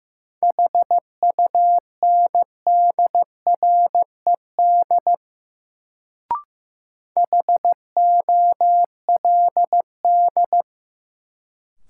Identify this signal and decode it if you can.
Morse code, heard as a keyed tone, about 15 wpm, 700 Hz. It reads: HUNDRED HOLD
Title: Sentences from Top 400 Words - New Word Drill - 15wpm